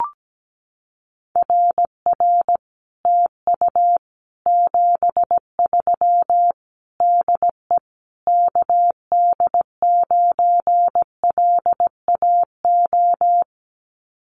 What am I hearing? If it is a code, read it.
Morse code, heard as a keyed tone, about 17 wpm, 700 Hz. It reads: RR TU 73 DE KD9LAO
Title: Common QSO Phrases Encore 17wpm